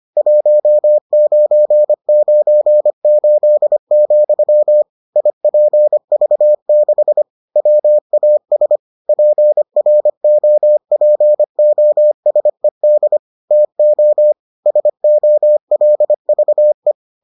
Morse code, heard as a keyed tone, about 25 wpm, 600 Hz. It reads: 1998, IPV6 WAS PROPOSED TO SOLVE